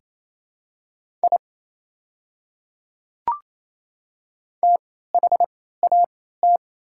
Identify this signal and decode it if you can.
Morse code, heard as a keyed tone, about 28 wpm, spaced as for 16 wpm, 700 Hz. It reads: I THAT
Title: 3x Character Spacing - Top 100 Words - 28wpm